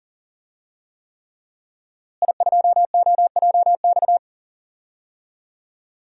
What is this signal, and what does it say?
Morse code, heard as a keyed tone, about 40 wpm, 700 Hz. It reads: I2OJX
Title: DXCC-100 Call Signs - 5 Characters - Repeated 3x - 40wpm